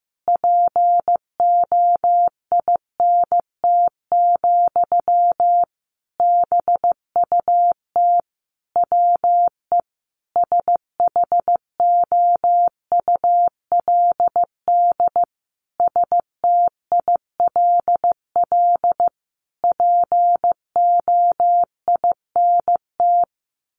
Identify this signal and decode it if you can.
Morse code, heard as a keyed tone, about 15 wpm, 700 Hz. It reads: POINT, BUT WE SHOULD STILL POINT
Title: Sentences from Top 1500 Words 15wpm